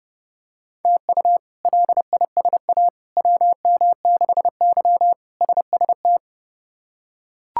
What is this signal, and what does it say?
Morse code, heard as a keyed tone, about 30 wpm, 700 Hz. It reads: TU LISA WM6Y SST
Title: K1USN SST QSO Practice - 30wpm